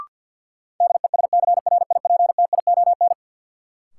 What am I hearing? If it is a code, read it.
Morse code, heard as a keyed tone, about 50 wpm, 700 Hz. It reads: DESCRIPTION
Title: Sentences from Top 1300 Words - New Word Drill - 50wpm